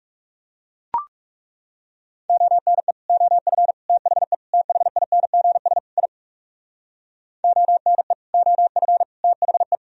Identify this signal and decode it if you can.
Morse code, heard as a keyed tone, about 37 wpm, 700 Hz. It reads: ONE OF THE THINGS I ONE OF THE
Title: Speed-Racing - Sets of 5 Words 30wpm